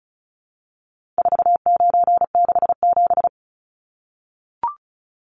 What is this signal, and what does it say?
Morse code, heard as a keyed tone, about 34 wpm, 700 Hz. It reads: 4967
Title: Numbers - 4-Digit 35wpm